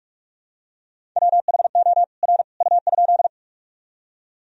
Morse code, heard as a keyed tone, about 46 wpm, 700 Hz. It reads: WHO R U?
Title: Sentences from Top 100 Words and Abbreviations 45wpm